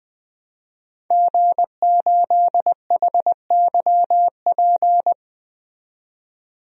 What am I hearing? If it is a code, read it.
Morse code, heard as a keyed tone, about 20 wpm, 700 Hz. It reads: G8HYP
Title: DXCC-100 Call Signs - 5 Characters - 20wpm